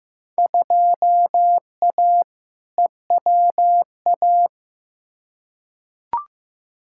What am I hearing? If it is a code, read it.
Morse code, heard as a keyed tone, about 15 wpm, 700 Hz. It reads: 2A EWA
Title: Field Day Exchanges - 15wpm